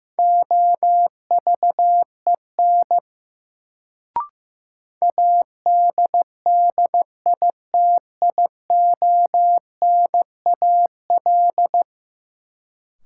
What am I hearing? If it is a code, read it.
Morse code, heard as a keyed tone, about 15 wpm, 700 Hz. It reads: OVEN ADDITIONAL
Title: Sentences from Top 1000 Words - Review Part 4 - New Word Drill - 15wpm